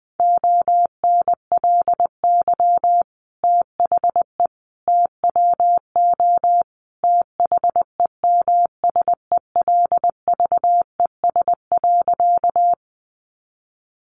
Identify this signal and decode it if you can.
Morse code, heard as a keyed tone, about 20 wpm, 700 Hz. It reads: ONLY THE TWO THEMSELVES.
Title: Sentences from Top 800 Words 20wpm